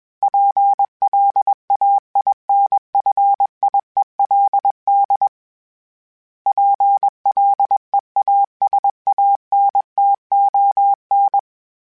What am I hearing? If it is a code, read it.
Morse code, heard as a keyed tone, about 21 wpm, 800 Hz. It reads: PLAINFIELD PLEASANTON
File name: fromSMs / FromSMs mp3s / Cities@20